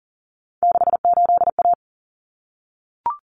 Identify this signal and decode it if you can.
Morse code, heard as a keyed tone, about 40 wpm, 700 Hz. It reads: 68A
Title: Common Cut Numbers - 3-Digit 40wpm